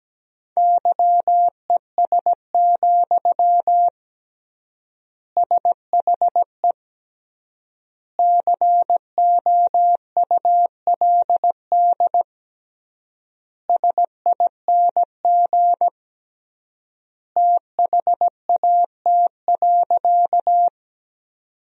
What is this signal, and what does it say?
Morse code, heard as a keyed tone, about 17 wpm, 700 Hz. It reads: YES, SHE COULD SING THAT.
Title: 3x Word-Spacing - Sentences from Top 500 Words - 17wpm